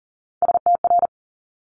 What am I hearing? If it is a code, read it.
Morse code, heard as a keyed tone, about 39 wpm, 700 Hz. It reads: STR
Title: Multi-letter Phonemes Quick Fire 40wpm